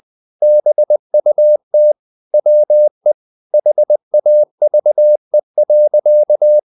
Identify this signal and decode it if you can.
Morse code, heard as a keyed tone, about 20 wpm, 600 Hz. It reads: BUT WE HAVE.
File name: the old man and the sea / the old man and the sea part 1 20wpm